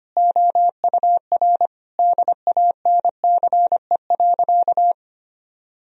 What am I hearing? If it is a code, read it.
Morse code, heard as a keyed tone, about 25 wpm, 700 Hz. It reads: OUR DANCE.
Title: Sentences from Top 1000 Words - Review Part 4 - 25wpm